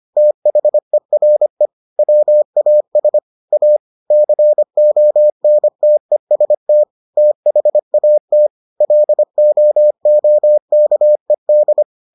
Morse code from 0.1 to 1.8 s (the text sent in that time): THERE